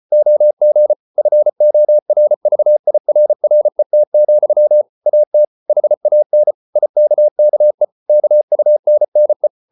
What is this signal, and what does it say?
Morse code, heard as a keyed tone, about 34 wpm, 600 Hz. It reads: OG FORVIRRET, AT HAN IKKE KUNNE
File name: Ferie QRQ 34wpm